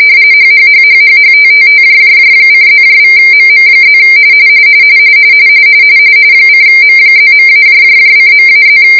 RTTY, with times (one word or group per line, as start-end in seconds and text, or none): 0.3-4.0 HHHHHHTTTJJJ:+=MM+
6.5-8.8 :M+.-:.-+